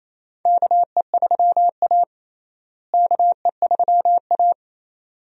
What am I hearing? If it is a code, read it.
Morse code, heard as a keyed tone, about 28 wpm, 700 Hz. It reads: KE3A KE3A
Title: USA Call Signs - 4 Characters - Repeated 2x - 28wpm